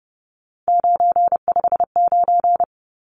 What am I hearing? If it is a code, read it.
Morse code, heard as a keyed tone, about 30 wpm, 700 Hz. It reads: 959
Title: Numbers - 3-Digit 30wpm